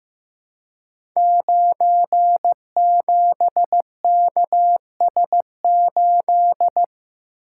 Morse code, heard as a keyed tone, about 15 wpm, 700 Hz. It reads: 97KS8
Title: Code Groups - 5 Letter:Numbers - 15wpm